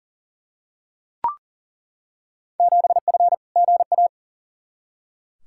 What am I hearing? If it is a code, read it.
Morse code, heard as a keyed tone, about 40 wpm, 700 Hz. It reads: 7F GA
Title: Field Day Exchanges Encore - 40wpm